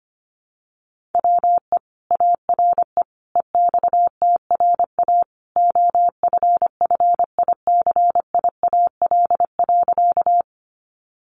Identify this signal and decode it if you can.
Morse code, heard as a keyed tone, about 25 wpm, 700 Hz. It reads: WE ARE EXTRA OFFICIAL.
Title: Sentences from Top 1000 Words 25wpm